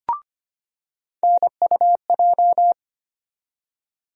Morse code, heard as a keyed tone, about 25 wpm, 700 Hz. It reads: NUJ
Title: USA Call Sign Postfixes - 3 Letters 25wpm